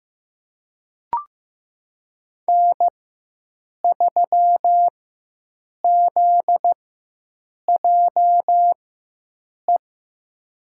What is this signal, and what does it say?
Morse code, heard as a keyed tone, about 15 wpm, spaced as for 7 wpm, 700 Hz. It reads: N3ZJE
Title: USA Call Signs - 5 Characters - 4x Character Spacing - 15wpm